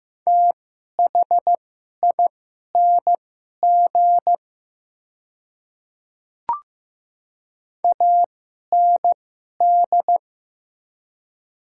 Morse code, heard as a keyed tone, about 15 wpm, spaced as for 11 wpm, 700 Hz. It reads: THING AND
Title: 2x Character Spacing - Top 100 Words - 15wpm